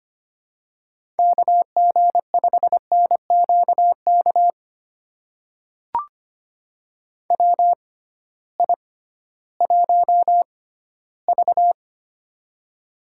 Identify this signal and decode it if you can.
Morse code, heard as a keyed tone, about 25 wpm, spaced as for 15 wpm, 700 Hz. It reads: KG5NQK WI1V
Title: USA Call Signs - 6x Character Spacing - 25wpm